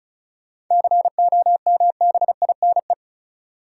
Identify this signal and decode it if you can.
Morse code, heard as a keyed tone, about 36 wpm, 700 Hz. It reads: COMBINE
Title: Sentences from Top 1600 Words - New Word Drill - 35wpm